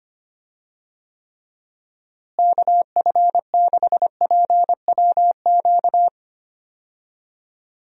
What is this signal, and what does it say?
Morse code, heard as a keyed tone, about 25 wpm, 700 Hz. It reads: KF6PWQ
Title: USA Call Signs - 6 Characters - Repeated 3x - 25wpm